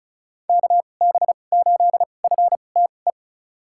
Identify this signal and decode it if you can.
Morse code, heard as a keyed tone, about 34 wpm, spaced as for 25 wpm, 700 Hz. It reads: KB8FTE
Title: USA Call Signs - 6 Characters - 2x Character Spacing - 35wpm